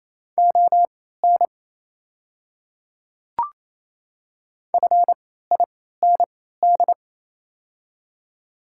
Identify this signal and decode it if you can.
Morse code, heard as a keyed tone, about 27 wpm, spaced as for 16 wpm, 700 Hz. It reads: ON FIND
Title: 3x Character Spacing - Top 500 Words - 28wpm